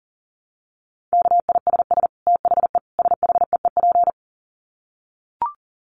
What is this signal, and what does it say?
Morse code, heard as a keyed tone, about 40 wpm, 700 Hz. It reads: KISS THE SHEEP
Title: Dit Challenge 40wpm